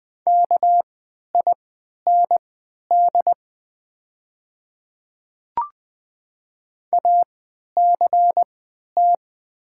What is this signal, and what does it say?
Morse code, heard as a keyed tone, about 20 wpm, spaced as for 11 wpm, 690 Hz. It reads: KIND ACT
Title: 3x Character Spacing - Top 300 Words - 20wpm